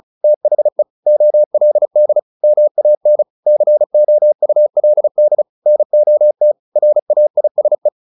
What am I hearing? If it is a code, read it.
Morse code, heard as a keyed tone, about 34 wpm, 600 Hz. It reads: THE OLD MAN COULD NOT RAISE
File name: the old man and the sea / the old man and the sea 1 35wpm